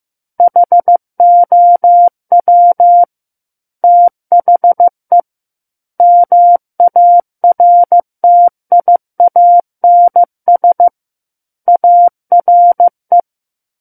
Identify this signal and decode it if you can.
Morse code, heard as a keyed tone, about 15 wpm, 700 Hz. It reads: HOW THE MARTIANS ARE
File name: Book One / 06-Chapter 6 - 15 WPM